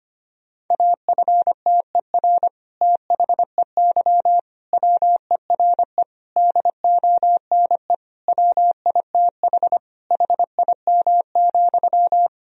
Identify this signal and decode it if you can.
Morse code, heard as a keyed tone, about 25 wpm, 700 Hz. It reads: AFTER THEY WERE DONE WITH HIM,
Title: Sentences from Top 1000 Words - Review Part 2 - 25wpm